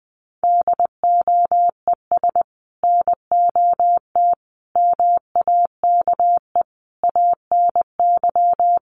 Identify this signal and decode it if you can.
Morse code, heard as a keyed tone, about 20 wpm, 700 Hz. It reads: DOES NOT MAKE ANY
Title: Sets of 5 Words Encore2 20wpm